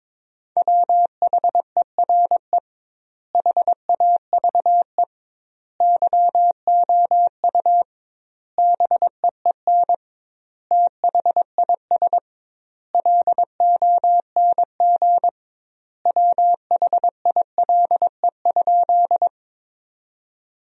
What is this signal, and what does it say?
Morse code, heard as a keyed tone, about 22 wpm, 700 Hz. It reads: WHERE HAVE YOU BEEN THIS LONG WHILE?
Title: 2x Word-Spacing - Sentences from Top 300 Words - 22wpm